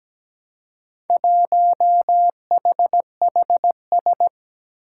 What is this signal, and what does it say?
Morse code, heard as a keyed tone, about 17 wpm, 700 Hz. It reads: 1HHS
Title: Four Challenging Letter-Number Combinations 17wpm